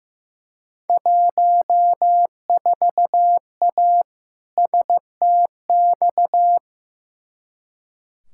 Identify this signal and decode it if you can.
Morse code, heard as a keyed tone, about 15 wpm, 700 Hz. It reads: 14A STX
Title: Field Day Exchanges Encore - 15wpm